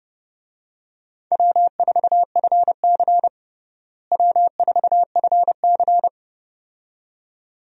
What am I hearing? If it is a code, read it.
Morse code, heard as a keyed tone, about 32 wpm, 700 Hz. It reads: W4FC W4FC
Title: USA Call Signs - 4 Characters - Repeated 2x - 30wpm